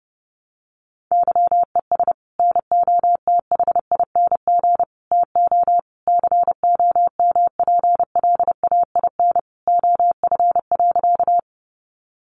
Morse code, heard as a keyed tone, about 30 wpm, 700 Hz. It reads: YES NOTHING TO COMPLAIN OF.
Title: Sentences from Top 1300 Words 30wpm